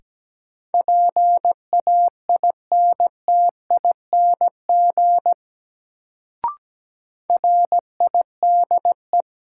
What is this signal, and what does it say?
Morse code, heard as a keyed tone, about 17 wpm, 700 Hz. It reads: PAINTING RIDE
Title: Sentences from Top 1200 Words - New Word Drill Alt - 17wpm